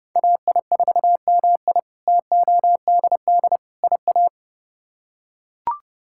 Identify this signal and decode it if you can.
Morse code, heard as a keyed tone, about 30 wpm, 700 Hz. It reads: AI4MI TODD IA
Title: K1USN SST QSO Practice - 30wpm